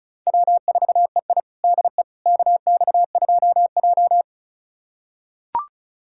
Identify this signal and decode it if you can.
Morse code, heard as a keyed tone, about 36 wpm, 700 Hz. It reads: W4EI DE KX2J
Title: USA Callsign DE Callsign - 4 Characters - 35wpm